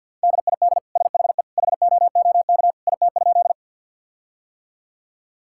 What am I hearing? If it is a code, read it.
Morse code, heard as a keyed tone, about 50 wpm, 700 Hz. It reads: DID SHE HOOK IT?
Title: Sentences from Top 1600 Words 50wpm